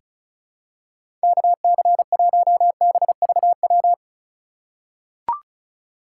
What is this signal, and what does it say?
Morse code, heard as a keyed tone, about 36 wpm, 700 Hz. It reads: KC1BVW
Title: USA Call Signs - 6 Characters - 6x Character Spacing - 35wpm